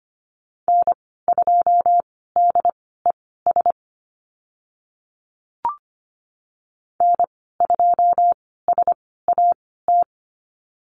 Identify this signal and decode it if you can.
Morse code, heard as a keyed tone, about 25 wpm, spaced as for 16 wpm, 700 Hz. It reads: N2DES N2SAT
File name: Lesson 15g - Number 5 - Call Signs 25wpm